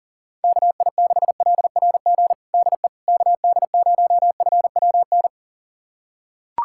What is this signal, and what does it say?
Morse code, heard as a keyed tone, about 40 wpm, 690 Hz. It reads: KI6LRG DE KD0FWN
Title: USA Callsign DE Callsign - 6 Characters - 40wpm